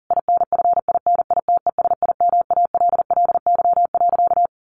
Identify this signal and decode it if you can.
Morse code, heard as a keyed tone, about 40 wpm, 700 Hz. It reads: INFINITESIMALLY.
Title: A Sound Of Thunder 40wpm